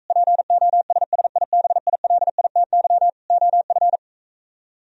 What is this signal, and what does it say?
Morse code, heard as a keyed tone, about 44 wpm, 700 Hz. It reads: POSSIBILITY OF